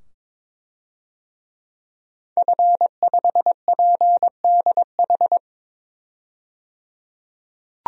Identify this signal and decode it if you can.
Morse code, heard as a keyed tone, about 21 wpm, 700 Hz. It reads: F5PDH
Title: DXCC-100 Call Signs - 5 Characters - 22wpm